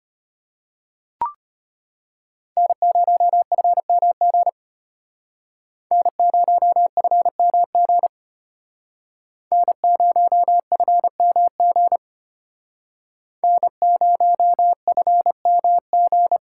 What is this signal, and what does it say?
Morse code, heard as a keyed tone, about 32 wpm, 700 Hz. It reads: N0FMG N0FMG N0FMG N0FMG